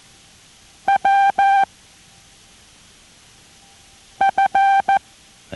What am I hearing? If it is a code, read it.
Morse code, heard as a keyed tone, about 14 wpm, spaced as for 3 wpm, 780 Hz. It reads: WF